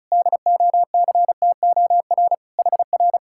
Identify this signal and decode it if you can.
Morse code, heard as a keyed tone, about 45 wpm, 700 Hz. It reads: DOCTOR HR